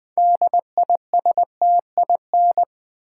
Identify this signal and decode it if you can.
Morse code, heard as a keyed tone, about 20 wpm, 700 Hz. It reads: DISTIN
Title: Build A Word - 3 Letters At A Time - 8-Letter Words - 20wpm